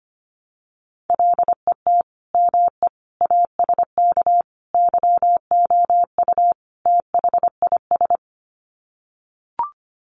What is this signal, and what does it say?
Morse code, heard as a keyed tone, about 25 wpm, 700 Hz. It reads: LET ME ASK YOU THIS
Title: Sets of 5 Words 25wpm